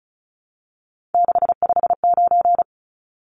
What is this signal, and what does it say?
Morse code, heard as a keyed tone, about 34 wpm, 700 Hz. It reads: 659
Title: Numbers - 3-Digit 35wpm